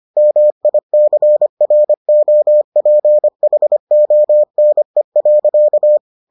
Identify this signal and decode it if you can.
Morse code, heard as a keyed tone, about 25 wpm, 600 Hz. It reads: MICROPHONE.